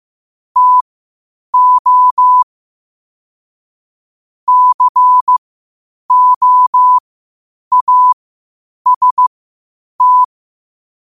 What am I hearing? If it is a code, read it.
Morse code, heard as a keyed tone, about 15 wpm, spaced as for 8 wpm, 1000 Hz. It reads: TO COAST
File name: qotdfiles / 8wpm